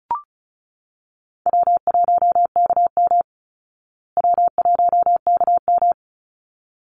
Morse code, heard as a keyed tone, about 34 wpm, 700 Hz. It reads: W1KM W1KM
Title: Easy Call Signs Repeated 2x - 35wpm